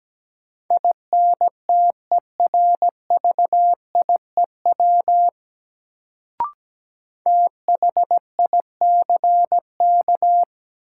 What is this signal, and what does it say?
Morse code, heard as a keyed tone, about 17 wpm, 700 Hz. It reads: INTERVIEW THICK
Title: Sentences from Top 1600 Words - New Word Drill - 17wpm